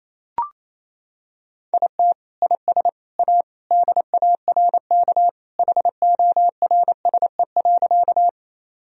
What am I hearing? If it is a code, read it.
Morse code, heard as a keyed tone, about 28 wpm, 700 Hz. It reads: IT IS A DARK HORSE.
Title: Sentences from Top 1000 Words - Review Part 1 - 28wpm